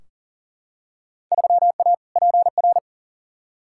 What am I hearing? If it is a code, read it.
Morse code, heard as a keyed tone, about 39 wpm, 700 Hz. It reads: 3A PR
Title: Field Day Exchanges Encore - 40wpm